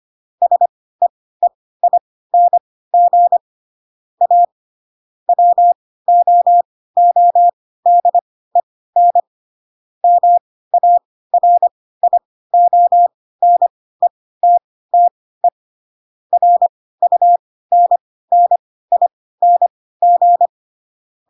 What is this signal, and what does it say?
Morse code, heard as a keyed tone, about 25 wpm, spaced as for 16 wpm, 700 Hz. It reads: SEEING A WOODEN MARIONETTE RUNNING
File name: Pinocchio / Pinocchio ch3 10 25 ik1hge